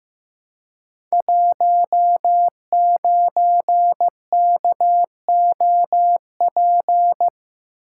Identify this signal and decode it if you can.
Morse code, heard as a keyed tone, about 15 wpm, 700 Hz. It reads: 19KOP
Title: Code Groups - 5 Letter:Numbers - 15wpm